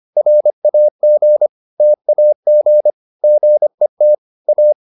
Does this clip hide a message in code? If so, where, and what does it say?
Yes: Morse code from 0.2 to 4.7 s, heard as a keyed tone, about 25 wpm, 600 Hz. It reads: RAG TAG GET A